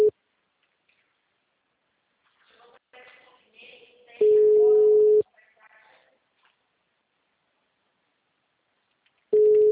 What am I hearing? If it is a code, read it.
Ringback tone (Europe, 425 Hz).
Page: Leme - SP